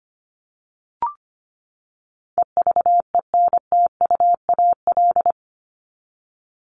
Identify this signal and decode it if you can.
Morse code, heard as a keyed tone, about 25 wpm, 700 Hz. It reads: EVENTUAL
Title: LY Suffix - Top 100 Words with Base Words 25wpm